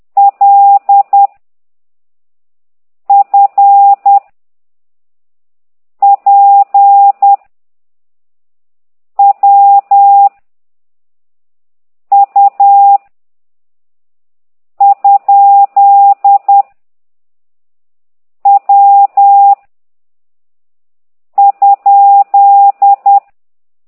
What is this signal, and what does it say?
Morse code, heard as a keyed tone, about 10 wpm, 800 Hz. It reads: L F P W U ? W ?
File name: ct2idw / exerc9